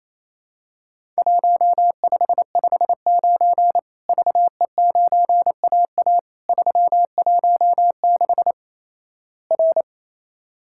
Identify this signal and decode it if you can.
Morse code, heard as a keyed tone, about 28 wpm, 700 Hz. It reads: 1559 VE9AA 316 R
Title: Worked All Europe DX Contest - QTC Exchange - 28wpm